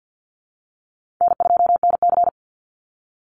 Morse code, heard as a keyed tone, about 50 wpm, 700 Hz. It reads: N2NC
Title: Easy Call Signs Repeated 2x - 50wpm